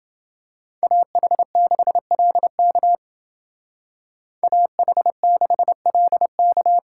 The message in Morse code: AH6LK AH6LK

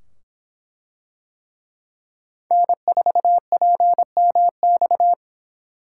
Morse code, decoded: N4PMX

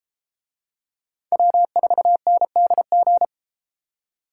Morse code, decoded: W4NDG

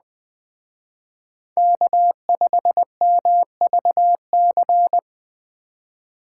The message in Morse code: K5MVC

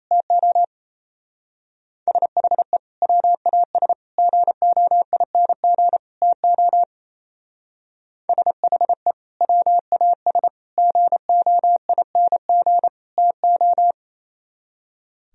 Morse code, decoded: TO SHE WAS GOING TO SHE WAS GOING TO

